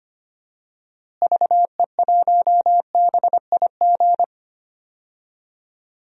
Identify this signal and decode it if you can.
Morse code, heard as a keyed tone, about 25 wpm, 700 Hz. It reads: VE1BIG
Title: DXCC-100 Call Signs - 6 Characters - Repeated 3x - 25wpm